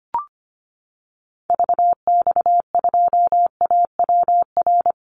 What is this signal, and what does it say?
Morse code, heard as a keyed tone, about 25 wpm, 700 Hz. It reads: VX2AWR
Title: Call Signs Repeated 2x - 25wpm